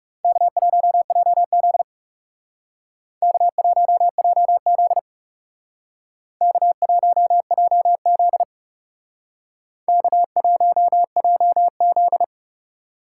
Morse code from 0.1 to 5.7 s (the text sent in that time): K1JZ K1JZ